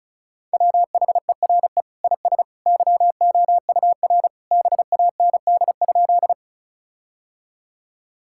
Morse code, heard as a keyed tone, about 36 wpm, 700 Hz. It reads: WHERE IS YOUR BAND?
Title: Sentences from Top 1600 Words 35wpm